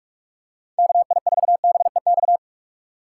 Morse code, heard as a keyed tone, about 42 wpm, 700 Hz. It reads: KI4BEX
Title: USA Call Signs - 6 Characters - 3x Character Spacing - 45wpm